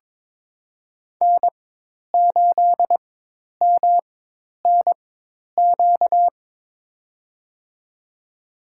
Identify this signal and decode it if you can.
Morse code, heard as a keyed tone, about 22 wpm, spaced as for 10 wpm, 700 Hz. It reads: N8MNQ
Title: USA Call Signs - 5 Characters - 4x Character Spacing - 22wpm